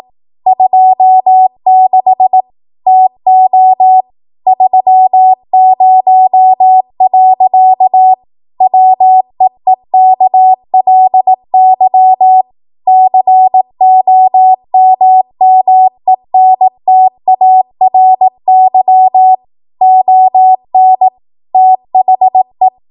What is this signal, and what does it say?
Morse code, heard as a keyed tone, about 18 wpm, 750 Hz. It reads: 26 TO 30. WEEKLY COMMENTARY ON THE
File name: Archive / 18 WPM / 18WPM